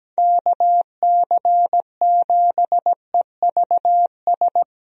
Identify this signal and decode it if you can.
Morse code, heard as a keyed tone, about 17 wpm, 700 Hz. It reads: KC7EVS
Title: USA Call Signs - 6 Characters - 17wpm